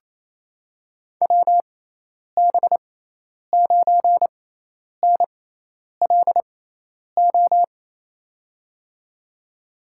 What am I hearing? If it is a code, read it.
Morse code, heard as a keyed tone, about 28 wpm, spaced as for 10 wpm, 700 Hz. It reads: WB9NLO
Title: USA Call Signs - 6 Characters - 6x Character Spacing - 28wpm